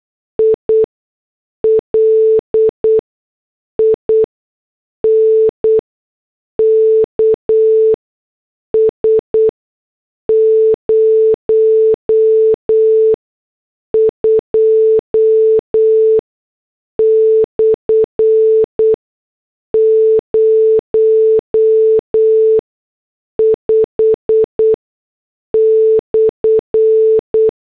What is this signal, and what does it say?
Morse code, heard as a keyed tone, about 8 wpm, spaced as for 6 wpm, 440 Hz. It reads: I LINKS 02/05/
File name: morse / headlines